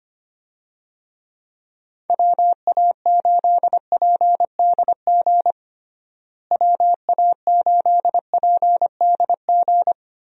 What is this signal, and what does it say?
Morse code, heard as a keyed tone, about 25 wpm, 700 Hz. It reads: WA8PDG WA8PDG
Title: USA Call Signs - 6 Characters - Repeated 3x - 25wpm